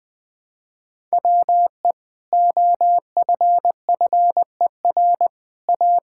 Morse code, heard as a keyed tone, about 20 wpm, 700 Hz. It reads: WE OFFER A
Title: Sentences with 3-Word Phrases - 20wpm